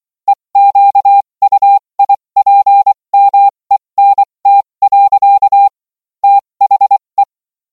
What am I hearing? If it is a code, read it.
Morse code, heard as a keyed tone, about 24 wpm, spaced as for 20 wpm, 800 Hz. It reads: EQUIPMENT. THE